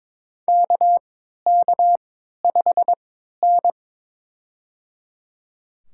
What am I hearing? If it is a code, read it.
Morse code, heard as a keyed tone, about 22 wpm, spaced as for 12 wpm, 700 Hz. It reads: KK5N
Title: USA Call Signs - 4 Characters - 3x Character Spacing - 22wpm